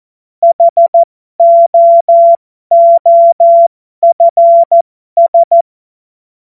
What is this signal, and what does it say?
Morse code, heard as a keyed tone, about 14 wpm, spaced as for 12 wpm, 680 Hz. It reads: HOOFS